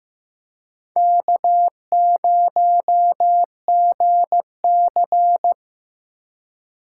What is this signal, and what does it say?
Morse code, heard as a keyed tone, about 18 wpm, 700 Hz. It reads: K0GC
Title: Speed Racing - USA 1x2 Call Signs 15wpm